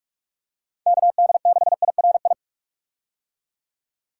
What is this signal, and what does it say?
Morse code, heard as a keyed tone, about 42 wpm, 700 Hz. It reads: KD6IRI
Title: K1USN SST QSO Practice - 45wpm